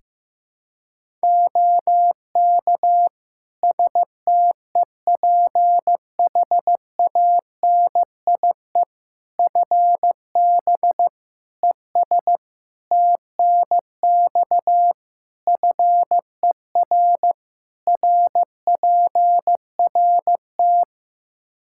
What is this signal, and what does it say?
Morse code, heard as a keyed tone, about 15 wpm, 700 Hz. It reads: OK STEPHANIE FB ES TNX FER RPRT
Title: Practice QSO - Exchange 2 - Info - 15wpm